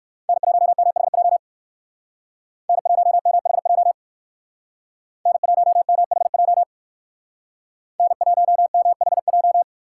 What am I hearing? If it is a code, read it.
Morse code, heard as a keyed tone, about 56 wpm, 700 Hz. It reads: N1MHJ N1MHJ N1MHJ N1MHJ
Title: Speed Racing - USA Call Signs 45wpm